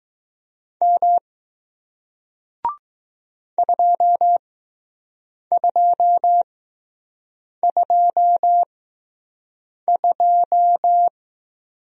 Morse code, heard as a keyed tone, about 20 wpm, spaced as for 6 wpm, 700 Hz. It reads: M 2222